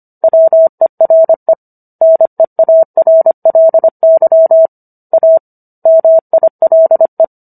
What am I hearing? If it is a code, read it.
Morse code, heard as a keyed tone, about 25 wpm, 650 Hz. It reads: WERE NEARLY A MILE